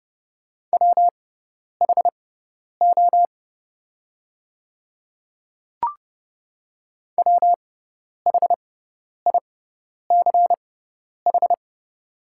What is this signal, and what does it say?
Morse code, heard as a keyed tone, about 30 wpm, spaced as for 10 wpm, 700 Hz. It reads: WHO WHICH
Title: 6x Character Spacing - Top 100 Words - 30wpm